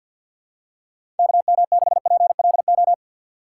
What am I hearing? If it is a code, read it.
Morse code, heard as a keyed tone, about 49 wpm, 700 Hz. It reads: KM6PLO
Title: USA Call Signs - 6 Characters - 4x Character Spacing - 50wpm